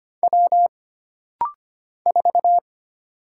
Morse code, heard as a keyed tone, about 25 wpm, 700 Hz. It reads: W 4
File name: Single Character (Letter:Number:Punctuation:Prosign) - Rapid-Fire - 25wpm